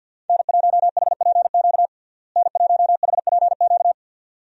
Morse code, heard as a keyed tone, about 48 wpm, 700 Hz. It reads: N1HPQ N1HPQ
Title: USA Call Signs - 5 Characters - Repeated 2x - 50wpm